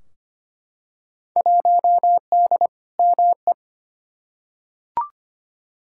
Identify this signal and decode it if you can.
Morse code, heard as a keyed tone, about 25 wpm, 700 Hz. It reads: 1D ME
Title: Field Day Exchanges Encore - 25wpm